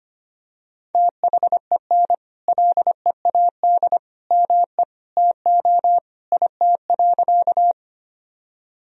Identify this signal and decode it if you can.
Morse code, heard as a keyed tone, about 25 wpm, 700 Hz. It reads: THEN LEAD ME TO IT.
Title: Sentences from Top 1000 Words - Review Part 2 - 25wpm